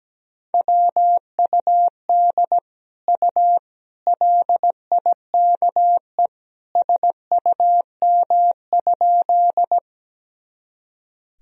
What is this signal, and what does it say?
Morse code, heard as a keyed tone, about 17 wpm, 700 Hz. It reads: WUD U LIKE SUM?
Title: Sentences from Top 100 Words and Abbreviations 17wpm